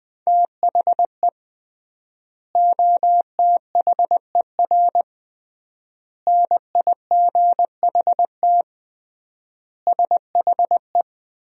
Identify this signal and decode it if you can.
Morse code, heard as a keyed tone, about 21 wpm, 700 Hz. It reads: THE OTHER NIGHT SHE
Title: 3x Word-Spacing - Sentences from Top 300 Words - 20wpm